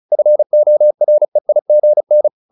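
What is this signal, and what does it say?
Morse code, heard as a keyed tone, about 34 wpm, 600 Hz. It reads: FOREIGN
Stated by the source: Morse practice at 35 wpm from CW med Gnister, sending Ken Follett's novel Never in English.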